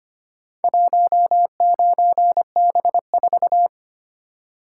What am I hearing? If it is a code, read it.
Morse code, heard as a keyed tone, about 25 wpm, 700 Hz. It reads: 19B4
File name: practice / 25wpm / Four Letter-Number Combinations - 25wpm